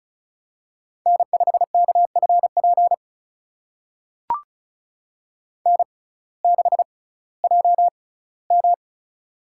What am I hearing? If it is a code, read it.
Morse code, heard as a keyed tone, about 36 wpm, spaced as for 19 wpm, 700 Hz. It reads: N5KFP N6JM
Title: USA Call Signs - 6x Character Spacing - 35wpm